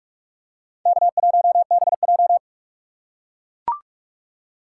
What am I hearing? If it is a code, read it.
Morse code, heard as a keyed tone, about 44 wpm, 700 Hz. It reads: K1BJ
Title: USA Call Signs - 4 Characters - 4x Character Spacing - 45wpm